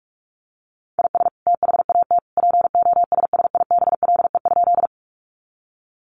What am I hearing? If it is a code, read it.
Morse code, heard as a keyed tone, about 44 wpm, 700 Hz. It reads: IS THAT POSSIBLE?
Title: Sentences from Top 500 Words 45wpm